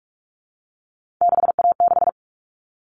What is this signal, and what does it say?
Morse code, heard as a keyed tone, about 44 wpm, 700 Hz. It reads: 6A6
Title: All Cut Numbers - 3-Digit 45wpm